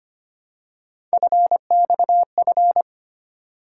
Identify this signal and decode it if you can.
Morse code, heard as a keyed tone, about 25 wpm, 700 Hz. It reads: FXF
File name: Three Letter-Number Combinations - 25wpm